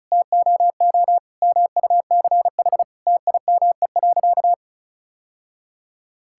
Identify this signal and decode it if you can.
Morse code, heard as a keyed tone, about 36 wpm, 700 Hz. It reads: TOO MUCH TIME.
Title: Sentences from Top 1000 Words - Review Part 6 - 35wpm